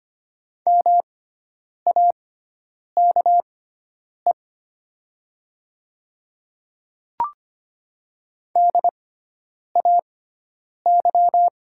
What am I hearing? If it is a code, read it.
Morse code, heard as a keyed tone, about 25 wpm, spaced as for 9 wpm, 700 Hz. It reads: MAKE DAY
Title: 6x Character Spacing - Top 200 Words - 25wpm